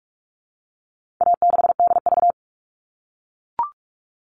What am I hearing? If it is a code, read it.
Morse code, heard as a keyed tone, about 44 wpm, 700 Hz. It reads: A6DV
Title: All Cut Numbers - 4-Digit 45wpm